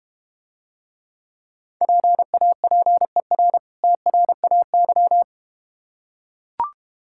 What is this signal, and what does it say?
Morse code, heard as a keyed tone, about 31 wpm, 690 Hz. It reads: PAPER TRAY
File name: practice / 25wpm / Difficult Words - Speed-Racing with Hints - 25wpm